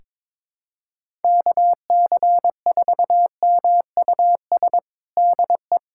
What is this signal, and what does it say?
Morse code, heard as a keyed tone, about 22 wpm, 700 Hz. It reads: KC4MUS DE
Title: USA Callsign DE Callsign 22wpm